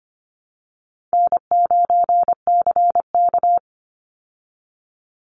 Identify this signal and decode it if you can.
Morse code, heard as a keyed tone, about 25 wpm, 700 Hz. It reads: N9CK
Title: Easy Call Signs Repeated 3x - 25wpm